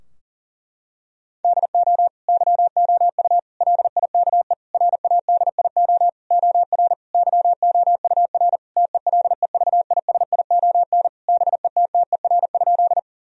Morse code, heard as a keyed tone, about 40 wpm, 700 Hz. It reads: DO YOU LIKE RADIO OR YOUR TELEVISION BETTER?
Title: Sentences from Top 1000 Words - Review Part 4 - 40wpm